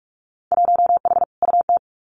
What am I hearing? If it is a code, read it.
Morse code, heard as a keyed tone, about 46 wpm, 700 Hz. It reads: 1H UT
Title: Winter Field Day Exchanges 45wpm